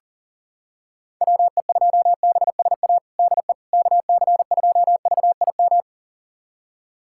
Morse code, heard as a keyed tone, about 40 wpm, 700 Hz. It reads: WE2BSA DE KC2VIM
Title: USA Callsign DE Callsign - 6 Characters - 40wpm